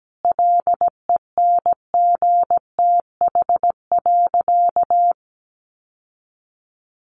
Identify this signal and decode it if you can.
Morse code, heard as a keyed tone, about 17 wpm, 700 Hz. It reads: LENGTH.
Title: Sentences from Top 1200 Words 17wpm